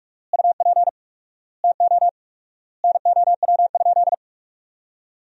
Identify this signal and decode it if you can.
Morse code, heard as a keyed tone, about 46 wpm, 700 Hz. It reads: UP TO NOW?
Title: 4x Word-Spacing - Sentences from Top 100 Words - 45wpm